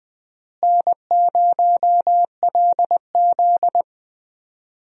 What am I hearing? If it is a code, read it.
Morse code, heard as a keyed tone, about 20 wpm, 700 Hz. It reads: N0LZ
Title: USA Call Signs - 4 Characters - Repeated 2x - 20wpm